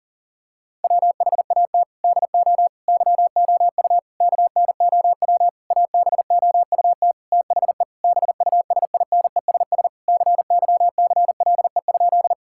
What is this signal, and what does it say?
Morse code, heard as a keyed tone, about 40 wpm, 700 Hz. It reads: WHAT DO YOU KNOW ABOUT THE BUSINESS CYCLE?
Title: Sentences from Top 1000 Words - Review Part 6 - 40wpm